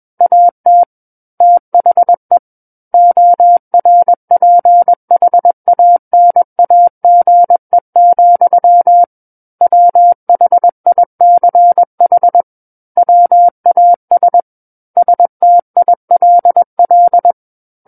Morse code, heard as a keyed tone, about 21 wpm, 700 Hz. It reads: AT THE ORPHANAGE, WHICH WAS STILL
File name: Book One / 12-Chapter 12 - 21 WPM